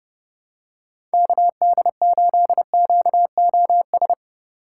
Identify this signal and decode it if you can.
Morse code, heard as a keyed tone, about 30 wpm, 700 Hz. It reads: KD8QOS
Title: USA Call Signs - 6 Characters - 4x Character Spacing - 30wpm